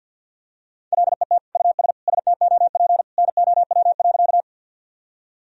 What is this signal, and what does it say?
Morse code, heard as a keyed tone, about 52 wpm, 700 Hz. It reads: LET US STOP NOW.
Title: Sentences from Top 1000 Words - Review Part 1 - 50wpm